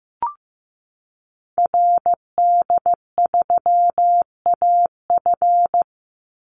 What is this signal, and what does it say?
Morse code, heard as a keyed tone, about 15 wpm, 700 Hz. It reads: RD3AF
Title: Call Signs Repeated 2x - 15wpm